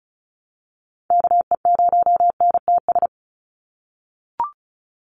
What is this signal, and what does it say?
Morse code, heard as a keyed tone, about 34 wpm, 700 Hz. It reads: KE0NTS
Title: USA Call Signs 35wpm